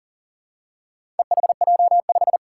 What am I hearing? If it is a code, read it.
Morse code, heard as a keyed tone, about 40 wpm, 700 Hz. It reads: EHJ5
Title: Four Challenging Letter-Number Combinations 40wpm